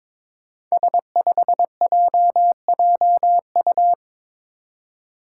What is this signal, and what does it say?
Morse code, heard as a keyed tone, about 22 wpm, 700 Hz. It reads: S5JJU
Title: Five Challenging Letter-Number Combinations 22wpm